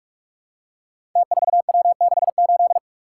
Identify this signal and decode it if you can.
Morse code, heard as a keyed tone, about 44 wpm, 700 Hz. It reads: T4W68
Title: Code Groups - 5 Letter:Numbers - 45wpm